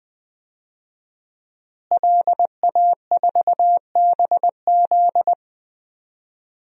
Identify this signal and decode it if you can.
Morse code, heard as a keyed tone, about 20 wpm, 700 Hz. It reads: LA4BZ
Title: DXCC-100 Call Signs - 5 Characters - Repeated 2x - 20wpm